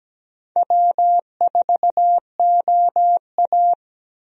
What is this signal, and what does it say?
Morse code, heard as a keyed tone, about 17 wpm, 700 Hz. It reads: W4OA
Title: USA Call Signs - 4 Characters - 17wpm